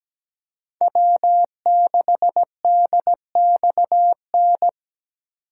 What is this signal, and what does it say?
Morse code, heard as a keyed tone, about 20 wpm, 700 Hz. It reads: W6DXN
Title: Speed Racing - USA 1x3 Call Signs 17wpm